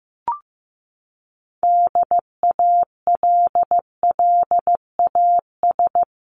Morse code, heard as a keyed tone, about 15 wpm, 700 Hz. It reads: DALLAS
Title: CWT Exchange 15wpm